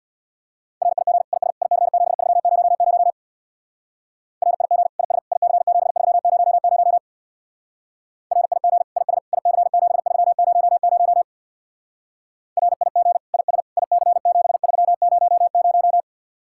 Morse code, heard as a keyed tone, about 66 wpm, 700 Hz. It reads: RIG IS IC7300 RIG IS IC7300 RIG IS IC7300 RIG IS IC7300